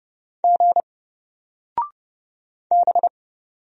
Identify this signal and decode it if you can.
Morse code, heard as a keyed tone, about 30 wpm, 700 Hz. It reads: G B
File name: Single-Letter-Number-30wpm